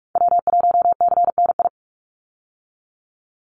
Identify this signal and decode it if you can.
Morse code, heard as a keyed tone, about 46 wpm, 700 Hz. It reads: W1CNI
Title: USA Call Signs 45wpm